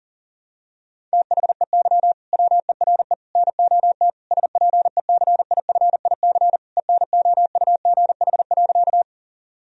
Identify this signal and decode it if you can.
Morse code, heard as a keyed tone, about 40 wpm, 700 Hz. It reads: THEY WERE NOT SPECIFIC ENOUGH.
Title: Sentences from Top 1000 Words - Review Part 3 - 40wpm